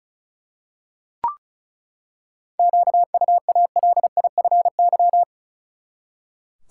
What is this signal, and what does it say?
Morse code, heard as a keyed tone, about 34 wpm, 700 Hz. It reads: QUALIFY
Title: Seven Letter Words - 35wpm